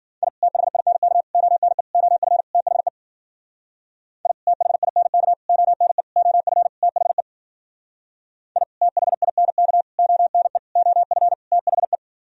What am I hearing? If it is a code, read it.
Morse code, heard as a keyed tone, about 52 wpm, 700 Hz. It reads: I THINK ONE OF THE I THINK ONE OF THE I THINK ONE OF THE